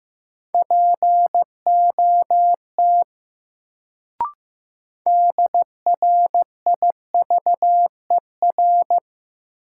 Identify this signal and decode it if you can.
Morse code, heard as a keyed tone, about 15 wpm, 700 Hz. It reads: POT DRIVER
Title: Sentences from Top 1000 Words - New Word Drill - 15wpm